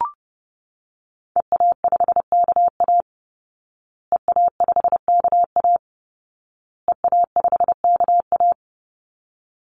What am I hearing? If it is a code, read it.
Morse code, heard as a keyed tone, about 30 wpm, 700 Hz. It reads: EA5KA EA5KA EA5KA
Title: Call Signs Repeated 2x - 30wpm